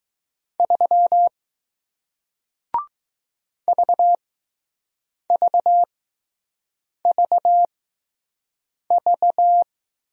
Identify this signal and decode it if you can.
Morse code, heard as a keyed tone, about 20 wpm, spaced as for 6 wpm, 700 Hz. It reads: 3 VVVV